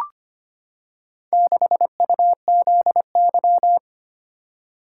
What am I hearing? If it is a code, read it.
Morse code, heard as a keyed tone, about 25 wpm, 700 Hz. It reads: 6UZY
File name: practice / 25wpm / Four Letter-Number Combinations - 25wpm